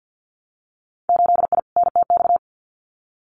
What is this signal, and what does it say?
Morse code, heard as a keyed tone, about 50 wpm, 700 Hz. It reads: 8I NTX
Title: Winter Field Day Exchanges Encore 50wpm